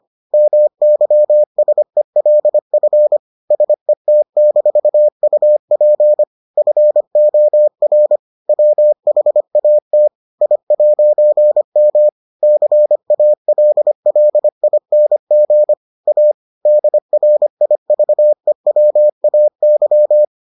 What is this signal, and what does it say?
Morse code, heard as a keyed tone, about 25 wpm, 600 Hz. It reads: MYSELF SET-UP FOR WHAT I'M CALLING A DRIVEWAY